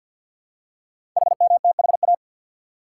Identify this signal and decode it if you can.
Morse code, heard as a keyed tone, about 46 wpm, 700 Hz. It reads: HMTHA
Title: Code Groups - 5 Letters - 50wpm